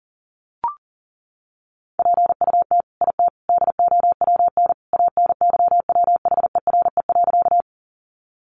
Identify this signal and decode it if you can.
Morse code, heard as a keyed tone, about 40 wpm, 700 Hz. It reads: PUT IT DOWN ANYWHERE.
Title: Sentences from Top 1400 Words 40wpm